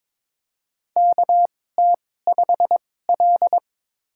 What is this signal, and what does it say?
Morse code, heard as a keyed tone, about 22 wpm, spaced as for 16 wpm, 700 Hz. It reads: KT5L
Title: USA Call Signs - 4 Characters - 2x Character Spacing - 22wpm